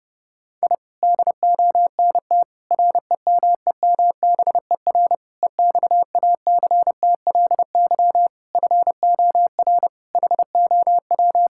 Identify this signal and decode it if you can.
Morse code, heard as a keyed tone, about 30 wpm, 700 Hz. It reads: I DONT REMEMBER EXACTLY FOR HOW
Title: Sentences from Top 1000 Words - Review Part 6 - 30wpm